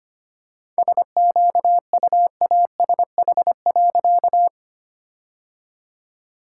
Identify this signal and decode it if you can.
Morse code, heard as a keyed tone, about 25 wpm, 700 Hz. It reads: SQUASH.
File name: practice / 25wpm / All-Age Appropriate Jokes - 25wpm